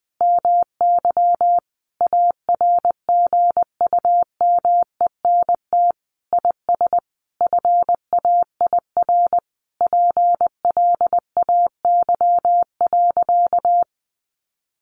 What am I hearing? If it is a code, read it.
Morse code, heard as a keyed tone, about 20 wpm, 700 Hz. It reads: MY ARGUMENT IS FAIR PLAY.
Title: Sentences from Top 1400 Words 20wpm